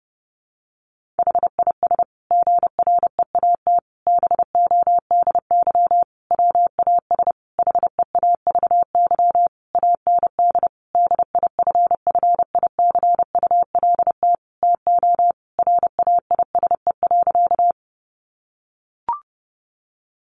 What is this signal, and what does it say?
Morse code, heard as a keyed tone, about 30 wpm, 700 Hz. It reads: HIS GREAT BODY WAS HEAVY AND DIFFICULT TO RAISE.
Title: Sentences from Top 1400 Words 30wpm